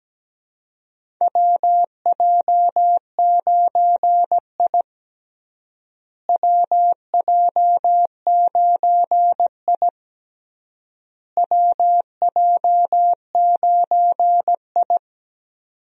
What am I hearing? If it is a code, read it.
Morse code, heard as a keyed tone, about 17 wpm, 700 Hz. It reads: WJ9I WJ9I WJ9I
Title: USA Call Signs - 4 Characters - Repeated 3x - 17wpm